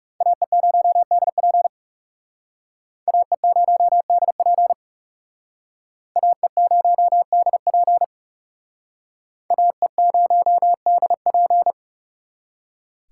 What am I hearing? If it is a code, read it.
Morse code, heard as a keyed tone, about 38 wpm, 700 Hz. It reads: AE0DP AE0DP AE0DP AE0DP
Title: Speed Racing - USA 2x2 Call Signs 30wpm